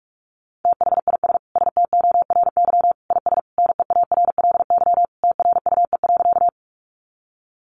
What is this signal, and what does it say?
Morse code, heard as a keyed tone, about 46 wpm, 700 Hz. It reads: THIS STORY IS NEARLY TRUE.
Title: Sentences from Top 1300 Words 45wpm